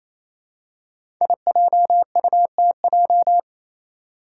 Morse code, heard as a keyed tone, about 28 wpm, 700 Hz. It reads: IJUTJ
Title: Five Challenging Letter-Number Combinations 28wpm